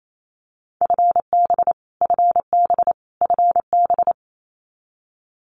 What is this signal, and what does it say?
Morse code, heard as a keyed tone, about 28 wpm, 700 Hz. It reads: FB FB FB
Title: Abbreviations Repeated 3x - 28wpm